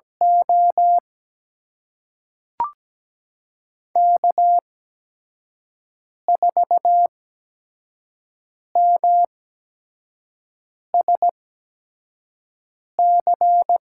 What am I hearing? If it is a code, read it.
Morse code, heard as a keyed tone, about 17 wpm, spaced as for 5 wpm, 700 Hz. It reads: O K4MSC
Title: USA Call Signs - 5 Characters - 8x Character Spacing - 17wpm